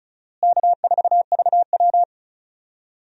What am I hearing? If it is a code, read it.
Morse code, heard as a keyed tone, about 36 wpm, 700 Hz. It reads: K4VW
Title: USA Call Signs - 4 Characters - Repeated 2x - 35wpm